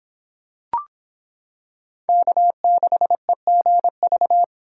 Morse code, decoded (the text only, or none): K6EGV